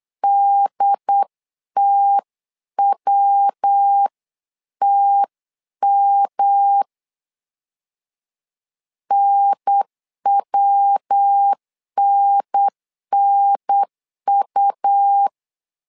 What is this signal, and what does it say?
Morse code, heard as a keyed tone, about 8 wpm, 800 Hz. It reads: DTWTM NWNNU
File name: LEZIONI / 3 MWRO / 3 MW 8wpm 2501